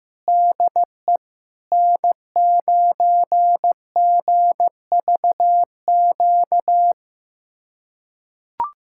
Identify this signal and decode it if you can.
Morse code, heard as a keyed tone, about 16 wpm, 700 Hz. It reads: DE N9GVQ
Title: USA Callsign DE Callsign 15wpm